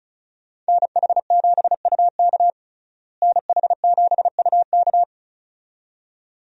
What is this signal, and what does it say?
Morse code, heard as a keyed tone, about 34 wpm, 700 Hz. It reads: NH7UK NH7UK
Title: USA Call Signs - 5 Characters - Repeated 2x - 35wpm